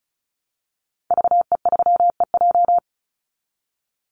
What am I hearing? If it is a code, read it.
Morse code, heard as a keyed tone, about 34 wpm, 700 Hz. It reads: VE3EJ
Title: Call Signs Repeated 3x - 35wpm